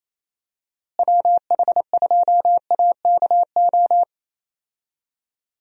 Morse code, heard as a keyed tone, about 28 wpm, 700 Hz. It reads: WH2AKO
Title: USA Call Signs - 6 Characters - Repeated 2x - 28wpm